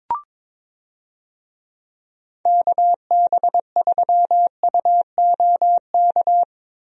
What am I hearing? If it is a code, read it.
Morse code, heard as a keyed tone, about 22 wpm, 700 Hz. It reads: KB3UOK
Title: USA Call Signs - 6 Characters - Repeated 3x - 22wpm